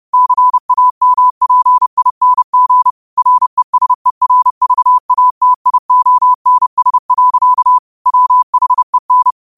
Morse code, heard as a keyed tone, about 30 wpm, 1000 Hz. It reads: CAMPING RESERVATIONS. WHEN
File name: qotdfiles / 30wpm